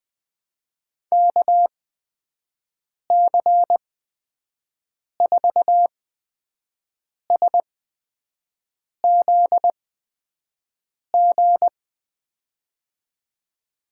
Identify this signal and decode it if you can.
Morse code, heard as a keyed tone, about 20 wpm, spaced as for 5 wpm, 700 Hz. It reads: KC4SZG
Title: USA Call Signs - 8x Character Spacing - 20wpm